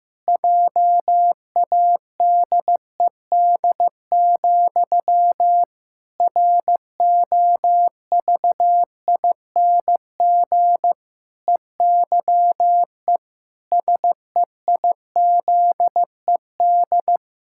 Morse code, heard as a keyed tone, about 15 wpm, 700 Hz. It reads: JADED, ROVING EYE SEIZED
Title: Pangram Challenge v2 - 15wpm